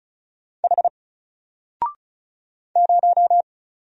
Morse code, heard as a keyed tone, about 36 wpm, 700 Hz. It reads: H 0
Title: Single Letter-Number Alt 35wpm